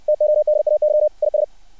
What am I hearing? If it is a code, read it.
Morse code, heard as a keyed tone, about 60 wpm, 600 Hz. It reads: TODAY IS
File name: archived / CRSnet